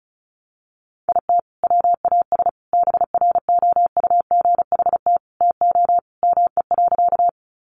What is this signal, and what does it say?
Morse code, heard as a keyed tone, about 34 wpm, 700 Hz. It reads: IT WAS BROUGHT TO ME.
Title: Sentences from Top 500 Words 35wpm